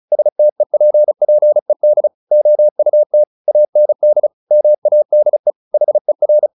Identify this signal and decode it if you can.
Morse code, heard as a keyed tone, about 34 wpm, 600 Hz. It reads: STEPPED OUT AND MADE HER
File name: the old man and the sea 2 35wpm